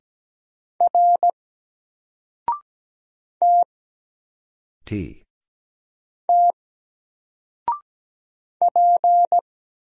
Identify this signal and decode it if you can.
Morse code, heard as a keyed tone, about 17 wpm, 700 Hz. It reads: R T T P
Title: Single Character (Letter:Number:Punctuation:Prosign) Alt 17wpm